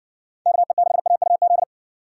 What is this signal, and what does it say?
Morse code, heard as a keyed tone, about 56 wpm, 700 Hz. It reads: KE6AUZ